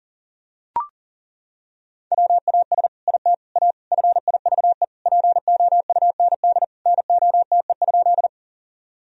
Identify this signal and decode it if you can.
Morse code, heard as a keyed tone, about 40 wpm, 700 Hz. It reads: WAS IT A FIVE POUND NOTE?
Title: Sentences from Top 1000 Words - Review Part 2 - 40wpm